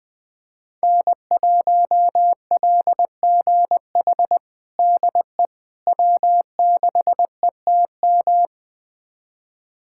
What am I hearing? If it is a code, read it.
Morse code, heard as a keyed tone, about 20 wpm, 700 Hz. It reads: N1LGH DE W6ETM
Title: USA Callsign DE Callsign - 5 Characters - 20wpm